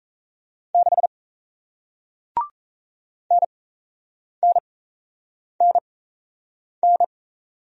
Morse code, heard as a keyed tone, about 38 wpm, spaced as for 8 wpm, 700 Hz. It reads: 6 NNNN